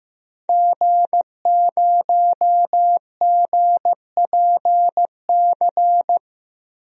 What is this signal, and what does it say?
Morse code, heard as a keyed tone, about 15 wpm, 700 Hz. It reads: G0GPC
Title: DXCC-100 Call Signs - 5 Characters - Repeated 3x - 15wpm